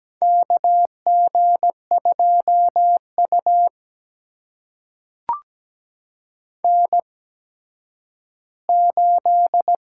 Morse code, heard as a keyed tone, about 17 wpm, spaced as for 10 wpm, 700 Hz. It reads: KG2U N8
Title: USA Call Signs - 4 Characters - 8x Character Spacing - 17wpm